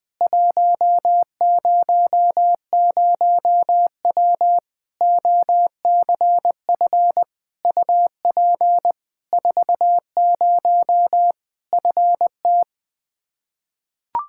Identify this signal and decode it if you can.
Morse code, heard as a keyed tone, about 20 wpm, 700 Hz. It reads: 100W OCF UP 40 FT
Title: K1USN SST QSO Practice 20wpm